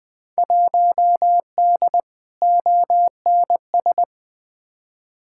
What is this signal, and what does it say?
Morse code, heard as a keyed tone, about 20 wpm, 700 Hz. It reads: 1D ONS
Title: Field Day Exchanges - 20wpm